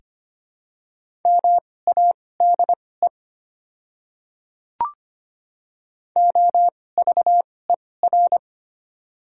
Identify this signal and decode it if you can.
Morse code, heard as a keyed tone, about 25 wpm, spaced as for 18 wpm, 690 Hz. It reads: MADE OVER
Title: 2x Character Spacing - Top 200 Words - 25wpm